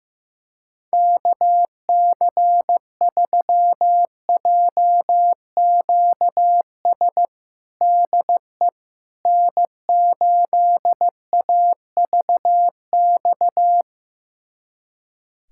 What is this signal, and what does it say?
Morse code, heard as a keyed tone, about 16 wpm, 700 Hz. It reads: KC3JQS DE N8AVX
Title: USA Callsign DE Callsign 15wpm